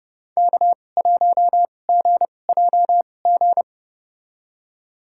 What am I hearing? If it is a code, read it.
Morse code, heard as a keyed tone, about 30 wpm, spaced as for 22 wpm, 700 Hz. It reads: K1GJG
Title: USA Call Signs - 2x Character Spacing - 30wpm